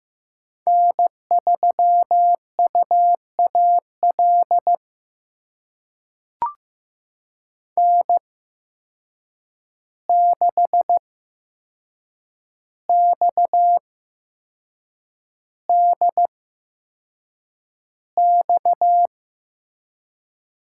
Morse code, heard as a keyed tone, about 15 wpm, spaced as for 6 wpm, 700 Hz. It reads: N3UAL N6XDX
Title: USA Call Signs - 5 Characters - 8x Character Spacing - 15wpm